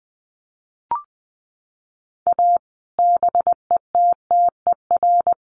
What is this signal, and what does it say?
Morse code, heard as a keyed tone, about 20 wpm, 700 Hz. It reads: A BETTER
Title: Sets of 5 Words Encore2 20wpm